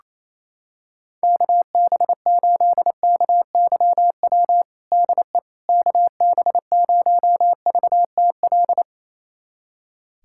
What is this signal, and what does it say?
Morse code, heard as a keyed tone, about 28 wpm, 700 Hz. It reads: KB8KYW DE KB0VTL
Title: USA Callsign DE Callsign 28wpm